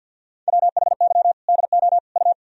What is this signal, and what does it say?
Morse code, heard as a keyed tone, about 46 wpm, 700 Hz. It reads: WHY DO U
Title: Sentences from Top 300 Words and Abbreviations 50wpm